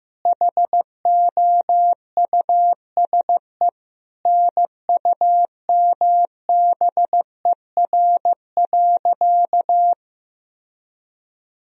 Morse code, heard as a keyed tone, about 15 wpm, 700 Hz. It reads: HOUSE NUMBER.
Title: Sentences from Top 1000 Words - Review Part 1 - 15wpm